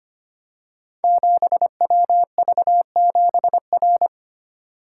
Morse code, heard as a keyed tone, about 25 wpm, 700 Hz. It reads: 7WV7R
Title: Code Groups - 5 Letter:Numbers - 25wpm